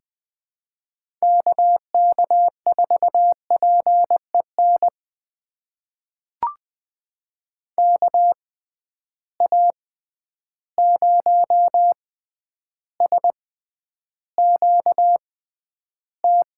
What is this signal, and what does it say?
Morse code, heard as a keyed tone, about 21 wpm, spaced as for 10 wpm, 700 Hz. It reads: KK4PEN KA0SQT
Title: USA Call Signs - 6 Characters - 6x Character Spacing - 20wpm